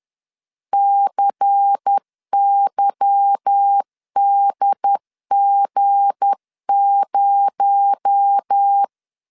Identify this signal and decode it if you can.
Morse code, heard as a keyed tone, about 10 wpm, 800 Hz. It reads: CYDG0
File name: LEZIONI / 8 YJ45 / 8 YJ4 10wpm 2633